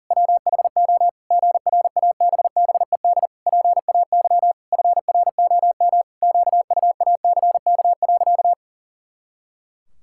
Morse code, heard as a keyed tone, about 40 wpm, 700 Hz. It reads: WHO GRABBED PAY FROM QUACK.
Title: Pangram Challenge v2 - 40wpm